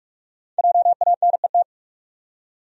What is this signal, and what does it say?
Morse code, heard as a keyed tone, about 46 wpm, 700 Hz. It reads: JANET
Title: USA Common Names - Max 5 Letters - 45wpm